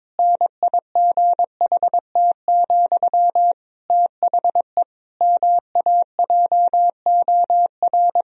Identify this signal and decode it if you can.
Morse code, heard as a keyed tone, about 22 wpm, 700 Hz. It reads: NIGHT, THE MAJOR
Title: Sentences from Top 1000 Words - Review Part 4 - 22wpm